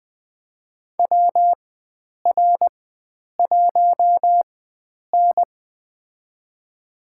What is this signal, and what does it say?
Morse code, heard as a keyed tone, about 20 wpm, spaced as for 9 wpm, 700 Hz. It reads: WR1N
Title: USA Call Signs - 4 Characters - 4x Character Spacing - 20wpm